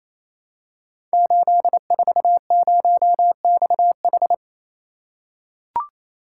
Morse code, heard as a keyed tone, about 27 wpm, 700 Hz. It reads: 840XH
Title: Code Groups - 5 Letter:Numbers - 28wpm